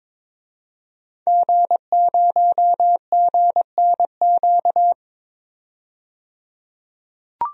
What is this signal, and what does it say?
Morse code, heard as a keyed tone, about 22 wpm, 700 Hz. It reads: G0GNQ
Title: DXCC-100 Call Signs - 5 Characters - 22wpm